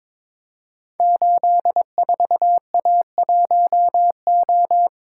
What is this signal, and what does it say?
Morse code, heard as a keyed tone, about 22 wpm, 700 Hz. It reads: 84A1O
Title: Code Groups - 5 Letter:Numbers - 22wpm